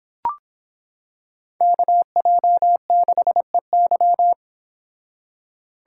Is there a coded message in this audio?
Yes: Morse code from 1.6 to 4.3 s, heard as a keyed tone, about 25 wpm, 700 Hz. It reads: KJ6EY